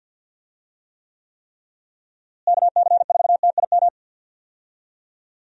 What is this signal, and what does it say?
Morse code, heard as a keyed tone, about 47 wpm, 700 Hz. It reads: KC4TIM
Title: USA Call Signs - 6 Characters - Repeated 3x - 50wpm